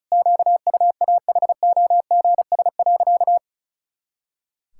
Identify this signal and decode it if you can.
Morse code, heard as a keyed tone, about 34 wpm, 700 Hz. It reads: QUAHOGS.
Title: Pangram Challenge v2 - 35wpm